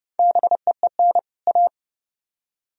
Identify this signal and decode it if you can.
Morse code, heard as a keyed tone, about 40 wpm, 700 Hz. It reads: BEEN A